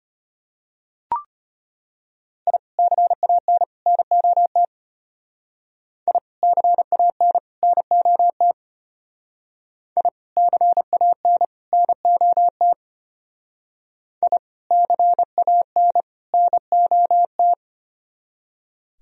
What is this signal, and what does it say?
Morse code, heard as a keyed tone, about 29 wpm, 700 Hz. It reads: I CAN NOT I CAN NOT I CAN NOT I CAN NOT
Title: Speed-Racing - Sets of 3 Words 25wpm